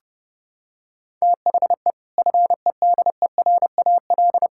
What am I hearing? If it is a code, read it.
Morse code, heard as a keyed tone, about 30 wpm, 700 Hz. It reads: THE FEDERAL
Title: Sentences from Top 1600 Words 30wpm